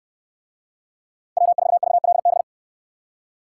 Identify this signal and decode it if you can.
Morse code, heard as a keyed tone, about 61 wpm, 700 Hz. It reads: W4VLB